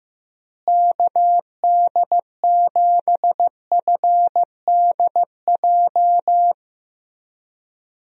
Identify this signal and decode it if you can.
Morse code, heard as a keyed tone, about 15 wpm, 700 Hz. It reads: KD7FDJ
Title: USA Call Signs - 6 Characters - Repeated 3x - 15wpm